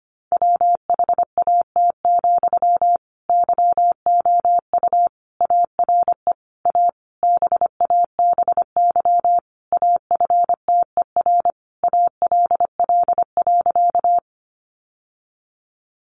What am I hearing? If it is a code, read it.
Morse code, heard as a keyed tone, about 25 wpm, 700 Hz. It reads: WHAT, YOU ARE A BABY AFTER ALL.
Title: Sentences from Top 1300 Words 25wpm